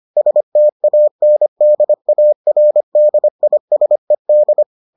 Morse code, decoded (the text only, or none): STANDARDISED